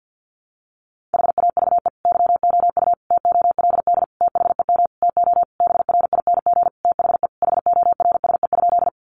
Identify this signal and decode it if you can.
Morse code, heard as a keyed tone, about 50 wpm, 700 Hz. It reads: HAVE YOU TOLD THEM TO BRING THE HORSE?
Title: Sentences from Top 500 Words 50wpm